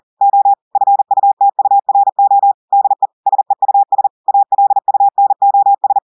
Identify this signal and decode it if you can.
Morse code, heard as a keyed tone, about 40 wpm, 800 Hz. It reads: O FUTURO DE SEUS ALUNOS
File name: audios / portugais / portugais texte en clair 1 vitesse 40 mots minutes